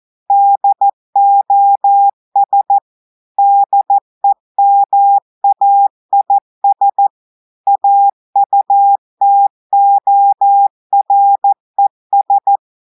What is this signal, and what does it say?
Morse code, heard as a keyed tone, about 14 wpm, 800 Hz. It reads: DOS DEMAIS AUTORES
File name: audios / portugais / portugais texte en clair 1 vitesse 14 mots minutes